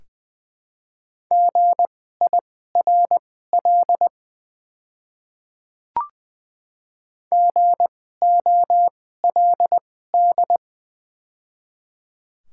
Morse code, heard as a keyed tone, about 20 wpm, spaced as for 14 wpm, 700 Hz. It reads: GIRL GOLD
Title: 2x Character Spacing - Top 500 Words - 20wpm